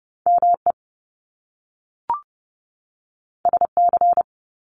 Morse code, heard as a keyed tone, about 30 wpm, 700 Hz. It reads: M E S C
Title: Top Two Letter Combinations in Words 30wpm